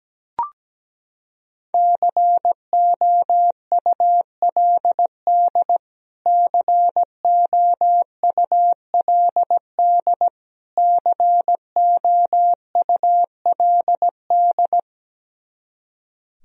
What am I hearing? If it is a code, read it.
Morse code, heard as a keyed tone, about 17 wpm, 700 Hz. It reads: COULD COULD COULD
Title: Top 100 Words Repeated 17wpm